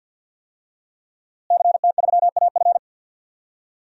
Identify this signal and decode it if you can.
Morse code, heard as a keyed tone, about 47 wpm, 700 Hz. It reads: CT3AF
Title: DXCC-100 Call Signs - 5 Characters - Repeated 2x - 50wpm